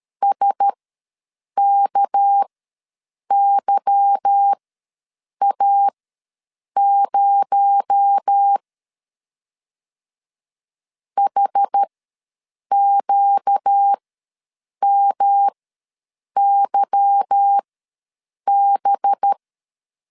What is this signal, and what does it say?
Morse code, heard as a keyed tone, about 13 wpm, spaced as for 7 wpm, 800 Hz. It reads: SKYA0 HQMYB
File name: LEZIONI / 8 YJ45 / 8 Y ws 12wpm 3409